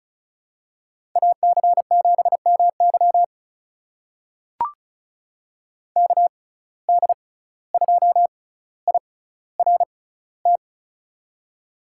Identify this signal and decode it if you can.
Morse code, heard as a keyed tone, about 34 wpm, spaced as for 17 wpm, 700 Hz. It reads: AC7MY KD2IRT
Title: USA Call Signs - 6x Character Spacing - 35wpm